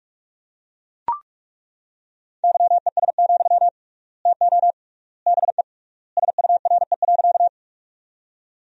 Morse code, heard as a keyed tone, about 41 wpm, 700 Hz. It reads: YES, TO BE SURE.
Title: 3x Word-Spacing - Sentences from Top 500 Words - 45wpm